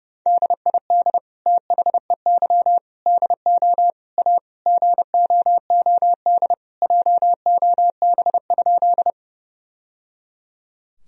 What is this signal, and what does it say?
Morse code, heard as a keyed tone, about 30 wpm, 700 Hz. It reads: DID THEY DO A GOOD JOB?
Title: Sentences from Top 1000 Words - Review Part 3 - 30wpm